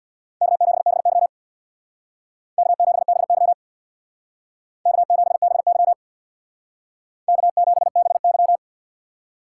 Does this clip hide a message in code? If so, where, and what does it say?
Yes: Morse code from 0.4 to 8.6 s, heard as a keyed tone, about 64 wpm, 700 Hz. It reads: K7BY K7BY K7BY K7BY